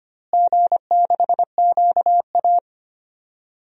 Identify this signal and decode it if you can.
Morse code, heard as a keyed tone, about 25 wpm, 700 Hz. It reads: G6QA
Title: DXCC-100 Call Signs - 4 Characters - Repeated 2x - 25wpm